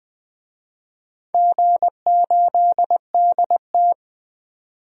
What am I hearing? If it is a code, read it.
Morse code, heard as a keyed tone, about 20 wpm, 700 Hz. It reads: G8DT
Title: DXCC-100 Call Signs - 4 Characters - 20wpm